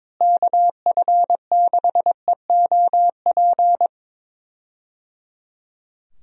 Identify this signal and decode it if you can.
Morse code, heard as a keyed tone, about 22 wpm, 700 Hz. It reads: KF6EOP
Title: USA Call Signs - 6 Characters - Repeated 3x - 22wpm